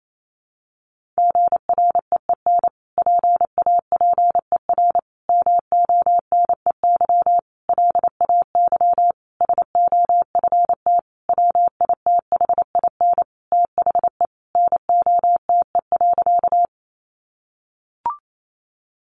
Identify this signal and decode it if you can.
Morse code, heard as a keyed tone, about 28 wpm, 700 Hz. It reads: GREEN PAPER MONEY LAY SOFT WITHIN THE NOTE.
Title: Sentences from Top 1500 Words 28wpm